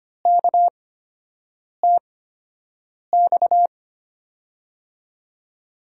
Morse code, heard as a keyed tone, about 25 wpm, spaced as for 7 wpm, 700 Hz. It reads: KTX
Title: USA Call Signs - 5 Characters - 8x Character Spacing - 25wpm